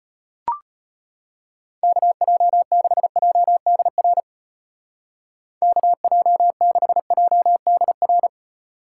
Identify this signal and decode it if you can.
Morse code, heard as a keyed tone, about 31 wpm, 700 Hz. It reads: KJ6JDR KJ6JDR
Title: Speed Racing - USA 2x3 Call Signs 25wpm